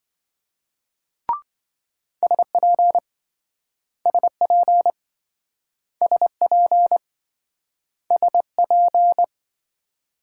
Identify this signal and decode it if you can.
Morse code, heard as a keyed tone, about 28 wpm, 700 Hz. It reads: SP SP SP SP